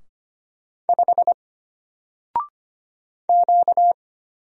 Morse code, heard as a keyed tone, about 25 wpm, 700 Hz. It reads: 5 Q
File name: Single-Letter-Number-25wpm